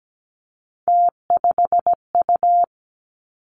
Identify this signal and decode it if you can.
Morse code, heard as a keyed tone, about 17 wpm, 700 Hz. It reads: T5U
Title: Three Challenging Letter-Number Combinations 17wpm